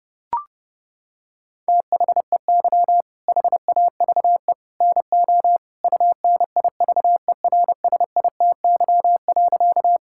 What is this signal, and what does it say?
Morse code, heard as a keyed tone, about 30 wpm, 700 Hz. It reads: THEY HAVE NO UNIVERSITY.
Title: Sentences from Top 1000 Words - Review Part 5 - 30wpm